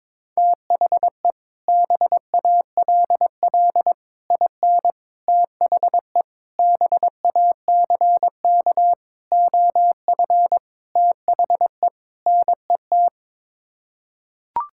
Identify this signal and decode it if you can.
Morse code, heard as a keyed tone, about 22 wpm, 700 Hz. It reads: THE BALL IN THE BACK OF THE NET
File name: practice / 22wpm / Sets of 8 Words - 22wpm